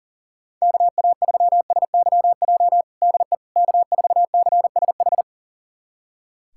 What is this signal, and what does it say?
Morse code, heard as a keyed tone, about 40 wpm, 700 Hz. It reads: KA3SYJ DE K4CSH
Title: USA Callsign DE Callsign 40wpm